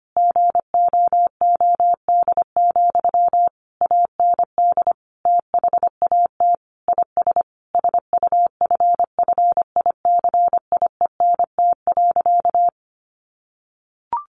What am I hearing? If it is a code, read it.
Morse code, heard as a keyed tone, about 25 wpm, 700 Hz. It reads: GOOD, AND THAT IS SUFFICIENT.
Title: Sentences from Top 1400 Words 25wpm